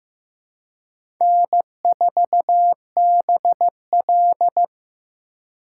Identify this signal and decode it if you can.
Morse code, heard as a keyed tone, about 15 wpm, 700 Hz. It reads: N4BL
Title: USA Call Signs - 4 Characters - 4x Character Spacing - 15wpm